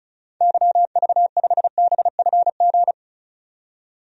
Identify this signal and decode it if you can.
Morse code, heard as a keyed tone, about 34 wpm, 700 Hz. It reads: YV5BFG
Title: DXCC-100 Call Signs - 6 Characters - 35wpm